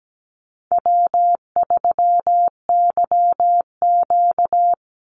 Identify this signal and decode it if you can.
Morse code, heard as a keyed tone, about 17 wpm, 700 Hz. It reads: W3YQ
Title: Easy Call Signs Repeated 2x - 17wpm